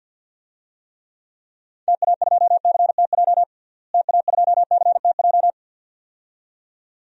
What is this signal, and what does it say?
Morse code, heard as a keyed tone, about 47 wpm, 700 Hz. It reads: TA2CTJ TA2CTJ
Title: DXCC-100 Call Signs - 6 Characters - Repeated 2x - 50wpm